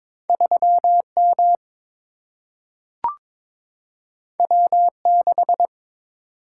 Morse code, heard as a keyed tone, about 22 wpm, 700 Hz. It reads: 3 M W 6
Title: Single-Letter Single-Number 22wpm